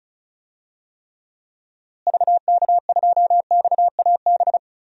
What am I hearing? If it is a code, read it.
Morse code, heard as a keyed tone, about 34 wpm, 700 Hz. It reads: VK2XAB
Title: DXCC-100 Call Signs - 6 Characters - Repeated 2x - 35wpm